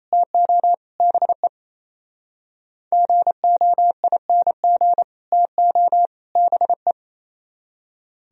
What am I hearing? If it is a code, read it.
Morse code, heard as a keyed tone, about 32 wpm, 700 Hz. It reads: TO BE GOING TO BE
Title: Speed-Racing - Sets of 3 Words 28wpm